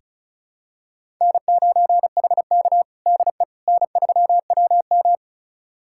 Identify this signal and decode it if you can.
Morse code, heard as a keyed tone, about 36 wpm, 700 Hz. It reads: N9HK DE N3WM
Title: USA Callsign DE Callsign - 4 Characters - 35wpm